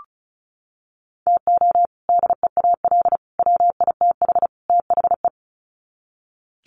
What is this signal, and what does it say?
Morse code, heard as a keyed tone, about 34 wpm, 700 Hz. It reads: TO DEAL WITH THE
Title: Sets of 4 Words 35wpm